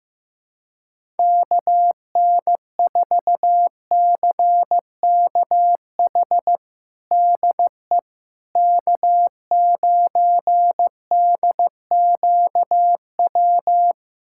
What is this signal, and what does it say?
Morse code, heard as a keyed tone, about 15 wpm, 700 Hz. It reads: KN4CKH DE K9DQW
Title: USA Callsign DE Callsign 15wpm